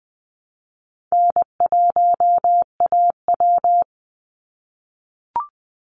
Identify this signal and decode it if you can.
Morse code, heard as a keyed tone, about 20 wpm, 700 Hz. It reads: N1AW
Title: NA SOTA Chaser Callsigns 20wpm